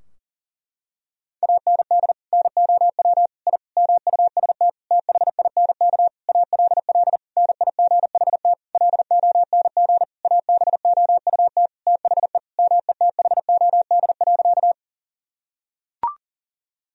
Morse code, heard as a keyed tone, about 40 wpm, 700 Hz. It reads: AND NOW I MUST THINK ALL NIGHT LONG ABOUT THE METHOD.
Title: Sentences from Top 1000 Words - Review Part 3 - 40wpm